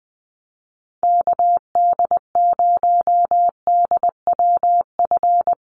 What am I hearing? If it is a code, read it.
Morse code, heard as a keyed tone, about 20 wpm, 700 Hz. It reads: KD0DWF
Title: USA Call Signs Repeated 2x 20wpm